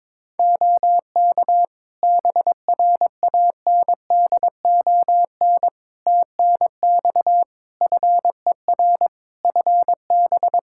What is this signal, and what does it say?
Morse code, heard as a keyed tone, about 22 wpm, 700 Hz. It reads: OK BRANDON TNX FER FB
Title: K1USN SST QSO Practice 22wpm